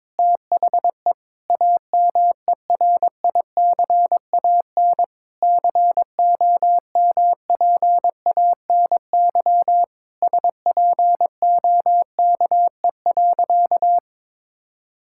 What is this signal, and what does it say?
Morse code, heard as a keyed tone, about 22 wpm, 700 Hz. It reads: THE AMERICAN COMPANY SPOKE.
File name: practice / 22wpm / Sentences with 3-Word Phrases - 22wpm